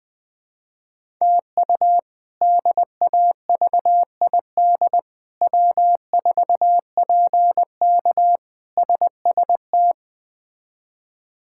Morse code, heard as a keyed tone, about 20 wpm, 700 Hz. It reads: TU DAVID W4PK SST
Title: K1USN SST QSO Practice - 20wpm